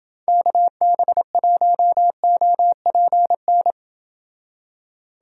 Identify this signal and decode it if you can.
Morse code, heard as a keyed tone, about 28 wpm, 700 Hz. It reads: KB1OPN